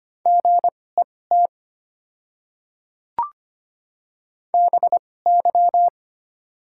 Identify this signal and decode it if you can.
Morse code, heard as a keyed tone, about 25 wpm, spaced as for 18 wpm, 700 Hz. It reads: GET BY
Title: 2x Character Spacing - Top 500 Words - 25wpm